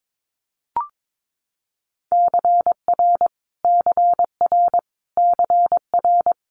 Morse code, heard as a keyed tone, about 22 wpm, 700 Hz. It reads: CR CR CR
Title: Multi-letter Phonemes Repeated 3x 22wpm